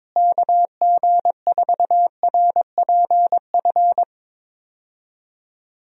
Morse code, heard as a keyed tone, about 22 wpm, 700 Hz. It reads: KG4RPF
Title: USA Call Signs - 6 Characters - Repeated 3x - 22wpm